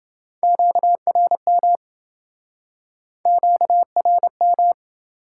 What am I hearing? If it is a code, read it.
Morse code, heard as a keyed tone, about 27 wpm, 700 Hz. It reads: QRM QRM